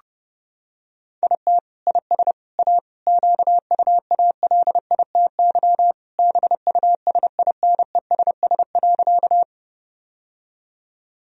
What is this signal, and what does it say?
Morse code, heard as a keyed tone, about 30 wpm, 700 Hz. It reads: IT IS A QUALITY BUSINESS.
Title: Sentences from Top 1000 Words - Review Part 4 - 30wpm